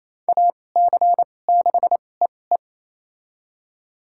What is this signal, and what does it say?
Morse code, heard as a keyed tone, about 28 wpm, spaced as for 20 wpm, 700 Hz. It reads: AC6EE
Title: USA Call Signs - 5 Characters - 2x Character Spacing - 28wpm